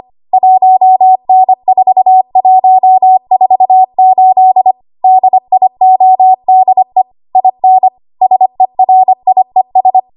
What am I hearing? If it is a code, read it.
Morse code, heard as a keyed tone, about 25 wpm, 750 Hz. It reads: 1N4148 DIODE IN SERIES
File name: Archive / 25 WPM / 25WPM